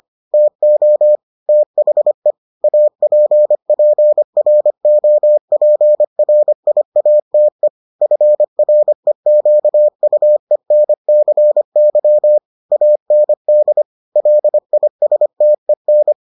Morse code, heard as a keyed tone, about 25 wpm, 600 Hz. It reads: TO THE APPROPRIATE FREQUENCY AND LISTEN